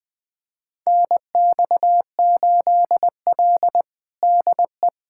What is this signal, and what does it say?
Morse code, heard as a keyed tone, about 20 wpm, 700 Hz. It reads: NX8L DE
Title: USA Callsign DE Callsign - 4 Characters - 20wpm